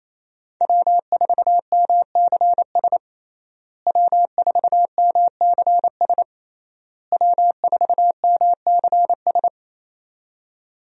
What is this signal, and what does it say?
Morse code, heard as a keyed tone, about 28 wpm, 700 Hz. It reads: W4MCS W4MCS W4MCS
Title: USA Call Signs - 5 Characters - Repeated 3x - 28wpm